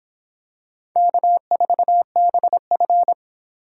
Morse code, heard as a keyed tone, about 26 wpm, 700 Hz. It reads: K4BF